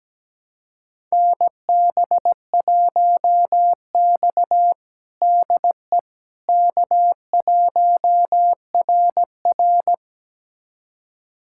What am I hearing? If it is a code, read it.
Morse code, heard as a keyed tone, about 17 wpm, 700 Hz. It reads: NB1X DE K1RR
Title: USA Callsign DE Callsign - 4 Characters - 17wpm